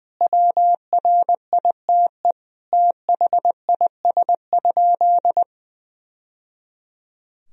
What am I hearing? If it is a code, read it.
Morse code, heard as a keyed tone, about 20 wpm, 700 Hz. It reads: WRITE THIS?
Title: Sentences from Top 1000 Words - Review Part 1 - 20wpm